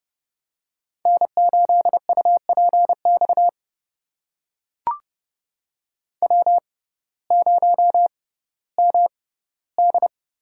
Morse code, heard as a keyed tone, about 30 wpm, spaced as for 17 wpm, 700 Hz. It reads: N8UPX W0MD
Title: USA Call Signs - 5 Characters - 6x Character Spacing - 30wpm